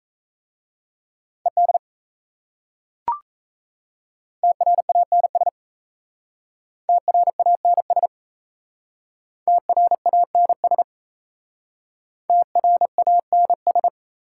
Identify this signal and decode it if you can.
Morse code, heard as a keyed tone, about 35 wpm, 700 Hz. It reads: ED TRANS TRANS TRANS TRANS